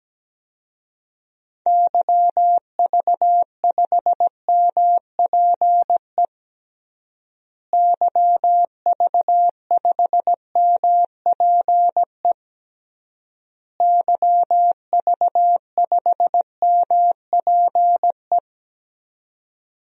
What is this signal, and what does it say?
Morse code, heard as a keyed tone, about 17 wpm, 700 Hz. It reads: YV5MPE YV5MPE YV5MPE
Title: DXCC-100 Call Signs - 6 Characters - Repeated 3x - 17wpm